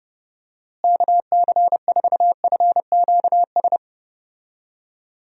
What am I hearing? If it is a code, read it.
Morse code, heard as a keyed tone, about 30 wpm, 700 Hz. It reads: KC4FQS
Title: USA Call Signs - 6 Characters - Repeated 3x - 30wpm